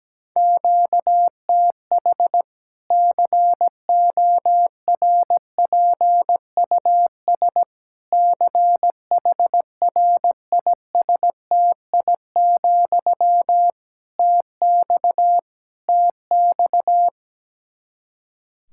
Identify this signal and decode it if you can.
Morse code, heard as a keyed tone, about 17 wpm, 700 Hz. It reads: QTH CORPUS CHRISTI, TX TX
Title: K1USN SST QSO Practice 17wpm